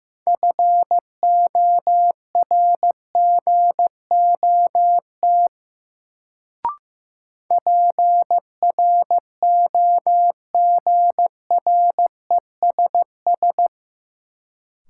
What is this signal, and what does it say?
Morse code, heard as a keyed tone, about 15 wpm, 700 Hz. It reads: FORGOT PROGRESS
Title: Sentences from Top 1400 Words - New Word Drill - 15wpm